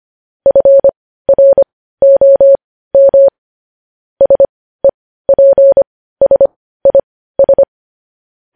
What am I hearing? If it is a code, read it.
Morse code, heard as a keyed tone, about 25 wpm, spaced as for 15 wpm, 550 Hz. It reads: FROM SEPSIS
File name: podcast / News headlines-1x-25x15